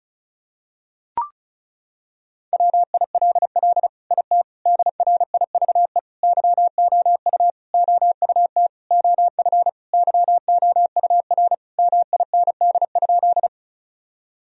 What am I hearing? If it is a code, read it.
Morse code, heard as a keyed tone, about 36 wpm, 700 Hz. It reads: WILL IT DRIVE YOU OUT OF YOUR MIND?
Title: Sentences from Top 1000 Words - Review Part 2 - 35wpm